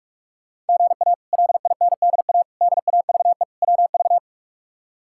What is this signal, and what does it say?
Morse code, heard as a keyed tone, about 46 wpm, 700 Hz. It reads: GA LINDA DAVE WV
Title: K1USN SST QSO Practice - 45wpm